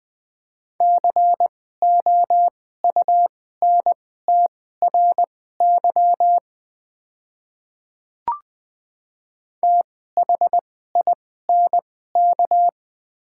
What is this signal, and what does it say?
Morse code, heard as a keyed tone, about 20 wpm, spaced as for 14 wpm, 700 Hz. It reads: COUNTRY THINK
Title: 2x Character Spacing - Top 300 Words - 20wpm